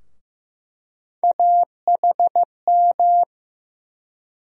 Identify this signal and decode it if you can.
Morse code, heard as a keyed tone, about 15 wpm, 700 Hz. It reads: AHM
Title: USA Call Sign Postfixes - 3 Letters 15wpm